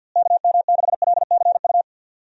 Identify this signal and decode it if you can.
Morse code, heard as a keyed tone, about 47 wpm, 700 Hz. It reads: KM6LCU
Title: USA Call Signs - 6 Characters - Repeated 3x - 50wpm